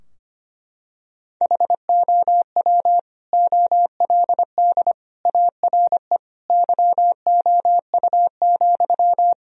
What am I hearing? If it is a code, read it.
Morse code, heard as a keyed tone, about 25 wpm, 700 Hz. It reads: HOW OLD ARE YOU,
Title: Sentences from Top 1000 Words - Review Part 4 - 25wpm